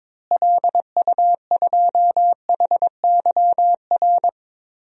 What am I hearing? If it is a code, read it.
Morse code, heard as a keyed tone, about 22 wpm, 700 Hz. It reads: LU2HYR
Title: DXCC-100 Call Signs - 6 Characters - Repeated 3x - 22wpm